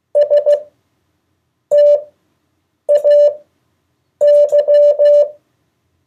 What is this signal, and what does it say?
Morse code, heard as a keyed tone, about 15 wpm, spaced as for 7 wpm, 580 Hz. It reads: STAY
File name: 2of4-words-1